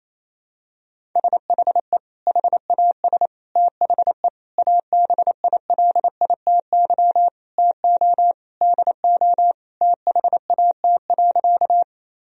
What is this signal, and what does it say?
Morse code, heard as a keyed tone, about 28 wpm, 700 Hz. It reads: SHE HAS THE ABILITY TO DO THAT.
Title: Sentences from Top 1000 Words - Review Part 3 - 28wpm